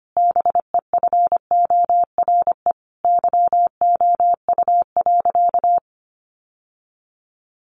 Morse code, heard as a keyed tone, about 25 wpm, 700 Hz. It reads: BEFORE YOU.
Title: Sentences from Top 1400 Words 25wpm